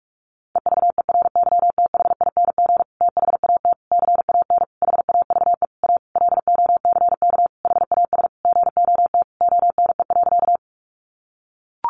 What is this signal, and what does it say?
Morse code, heard as a keyed tone, about 44 wpm, 700 Hz. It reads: EVERYTHING THAT CAN HAVE A LOCK HAS GOT ONE.
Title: Sentences from Top 1300 Words 45wpm